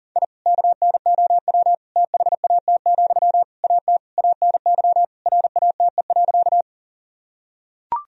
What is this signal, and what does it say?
Morse code, heard as a keyed tone, about 40 wpm, 700 Hz. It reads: I KNOW THAT, AT ANY RATE.
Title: Sentences from Top 1000 Words - Review Part 4 - 40wpm